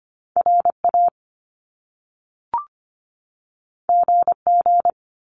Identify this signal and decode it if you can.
Morse code, heard as a keyed tone, about 25 wpm, 700 Hz. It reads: RA GG
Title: Top Two Letter Combinations in Words 25wpm